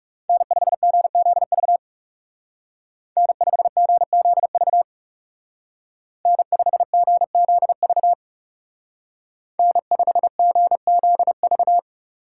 Morse code, heard as a keyed tone, about 37 wpm, 700 Hz. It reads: N5GZV N5GZV N5GZV N5GZV
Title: Speed Racing - USA Call Signs 30wpm